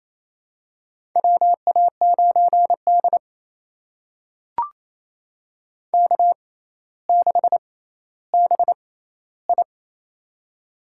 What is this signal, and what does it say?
Morse code, heard as a keyed tone, about 29 wpm, spaced as for 15 wpm, 700 Hz. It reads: WA9D K6BI
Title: USA Call Signs - 4 Characters - 6x Character Spacing - 28wpm